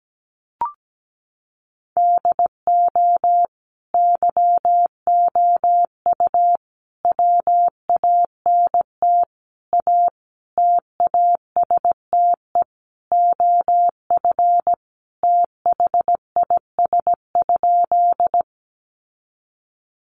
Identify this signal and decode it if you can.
Morse code, heard as a keyed tone, about 17 wpm, 700 Hz. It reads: DO YOU WANT A TASTE OF THIS?
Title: Sentences from Top 1100 Words 17wpm